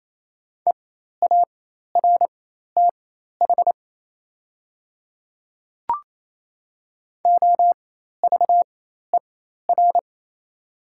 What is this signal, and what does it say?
Morse code, heard as a keyed tone, about 28 wpm, spaced as for 13 wpm, 700 Hz. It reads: EARTH OVER
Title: 4x Character Spacing - Top 500 Words - 28wpm